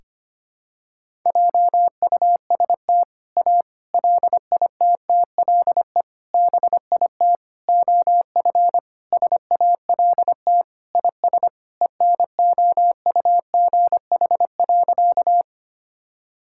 Morse code, heard as a keyed tone, about 25 wpm, 700 Hz. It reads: JUST A LITTLE BIT OF SALT IS ENOUGH.
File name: practice / 25wpm / Sentences with 4-Word Phrases - 25wpm